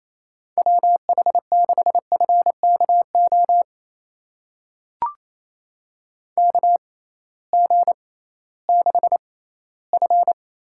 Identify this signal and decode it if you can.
Morse code, heard as a keyed tone, about 28 wpm, spaced as for 16 wpm, 700 Hz. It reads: WH6FKO KG6F
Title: USA Call Signs - 6 Characters - 6x Character Spacing - 28wpm